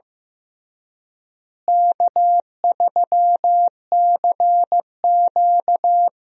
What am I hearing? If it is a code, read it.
Morse code, heard as a keyed tone, about 15 wpm, 700 Hz. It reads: K3CQ
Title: USA Call Signs - 4 Characters - Repeated 3x - 15wpm